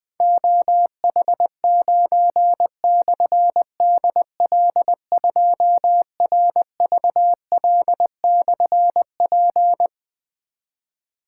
Morse code, heard as a keyed tone, about 20 wpm, 700 Hz. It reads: OH9/DL2RVL/P
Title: Difficult Call Signs 20wpm